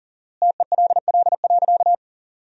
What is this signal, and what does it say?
Morse code, heard as a keyed tone, about 40 wpm, 700 Hz. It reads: TELL.
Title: 4x Word-Spacing - Sentences from Top 300 Words - 40wpm